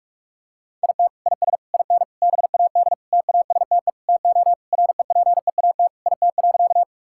Morse code, heard as a keyed tone, about 46 wpm, 700 Hz. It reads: IT IS IN BAD TASTE TO REPEAT IT.
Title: Sentences from Top 1600 Words 45wpm